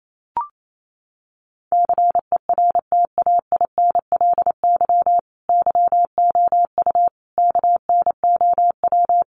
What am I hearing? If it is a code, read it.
Morse code, heard as a keyed tone, about 28 wpm, 700 Hz. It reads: CERTAINLY YOU KNOW
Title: Sentences from Top 900 Words 28wpm